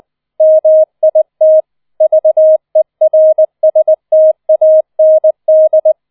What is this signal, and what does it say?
Morse code, heard as a keyed tone, about 20 wpm, 630 Hz. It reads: MIT VERSTAND
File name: 004 Vierter Streich 20wpm